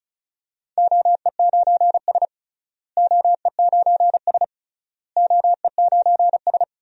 Morse code, heard as a keyed tone, about 34 wpm, 700 Hz. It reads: OE9S OE9S OE9S
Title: DXCC-100 Call Signs - 4 Characters - Repeated 3x - 35wpm